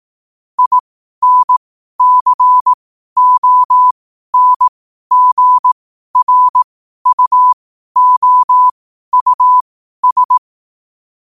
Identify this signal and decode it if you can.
Morse code, heard as a keyed tone, about 18 wpm, spaced as for 12 wpm, 1000 Hz. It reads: INCONGRUOUS
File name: qotdfiles / 12wpm